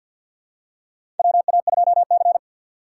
Morse code, heard as a keyed tone, about 53 wpm, 700 Hz. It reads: WA2C
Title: USA Call Signs - 4 Characters - 6x Character Spacing - 50wpm